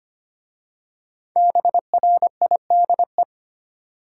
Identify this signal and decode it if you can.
Morse code, heard as a keyed tone, about 25 wpm, 700 Hz. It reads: BRIDE
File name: 5 Letter Words - First 3 Letters Given - 25wpm